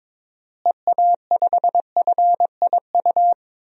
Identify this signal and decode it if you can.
Morse code, heard as a keyed tone, about 23 wpm, 700 Hz. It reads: EA5FIU
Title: DXCC-100 Call Signs - 6 Characters - Repeated 2x - 22wpm